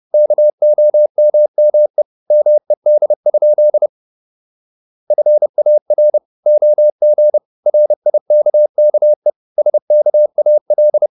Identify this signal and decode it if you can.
Morse code, heard as a keyed tone, about 30 wpm, 600 Hz. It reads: KOMME MED? FAR OG RIKKE SKAL
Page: Ferie 30 wpm